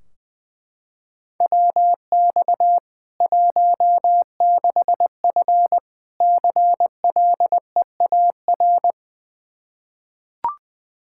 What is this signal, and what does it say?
Morse code, heard as a keyed tone, about 20 wpm, 700 Hz. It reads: WX 16F CLEAR
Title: Practice QSO - Exchange 2 - Info - 20wpm